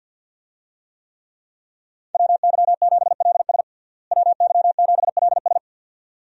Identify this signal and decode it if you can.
Morse code, heard as a keyed tone, about 48 wpm, 700 Hz. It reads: WY7LS WY7LS
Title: USA Call Signs - 5 Characters - Repeated 2x - 50wpm